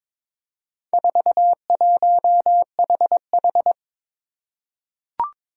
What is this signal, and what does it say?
Morse code, heard as a keyed tone, about 22 wpm, 700 Hz. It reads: 41HH
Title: Four Challenging Letter-Number Combinations 22wpm